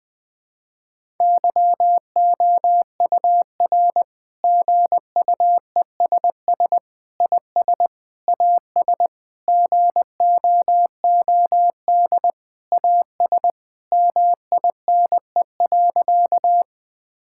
Morse code, heard as a keyed tone, about 21 wpm, 700 Hz. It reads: YOUR GUESS IS AS GOOD AS MINE.
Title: Sentences with 7-Word Phrases - 20wpm